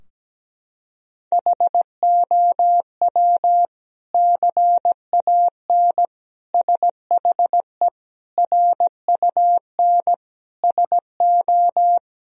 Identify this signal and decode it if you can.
Morse code, heard as a keyed tone, about 17 wpm, 700 Hz. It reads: HOW CAN SHE RUN SO
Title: Sentences from Top 1000 Words - Review Part 2 - 17wpm